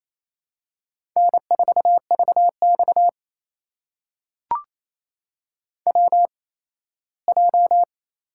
Morse code, heard as a keyed tone, about 28 wpm, spaced as for 17 wpm, 700 Hz. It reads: N4VX WJ
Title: USA Call Signs - 4 Characters - 8x Character Spacing - 28wpm